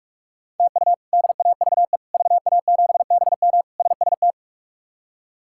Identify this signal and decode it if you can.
Morse code, heard as a keyed tone, about 41 wpm, 700 Hz. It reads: TU DAVE VA7BM SST
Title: K1USN SST QSO Practice - 45wpm